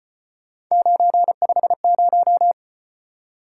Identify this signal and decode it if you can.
Morse code, heard as a keyed tone, about 32 wpm, 700 Hz. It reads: 950